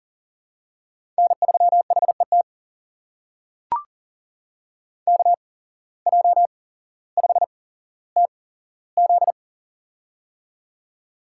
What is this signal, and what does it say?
Morse code, heard as a keyed tone, about 40 wpm, spaced as for 17 wpm, 700 Hz. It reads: N3HET KJ5TZ
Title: USA Call Signs - 5 Characters - 8x Character Spacing - 40wpm